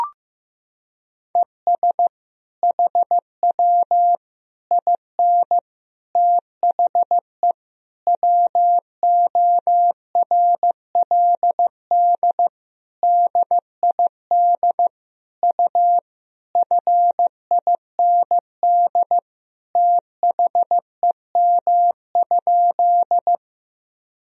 Morse code, heard as a keyed tone, about 15 wpm, 700 Hz. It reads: ES HW IN THE WORLD DID U FIND THEM?
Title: Sentences from Top 200 Words and Abbreviations 15wpm